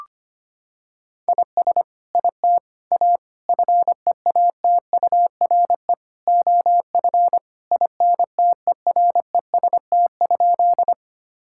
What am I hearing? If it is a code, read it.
Morse code, heard as a keyed tone, about 25 wpm, 700 Hz. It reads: IS IT A FEATURE OF INTEREST?
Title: Sentences from Top 1000 Words - Review Part 5 - 25wpm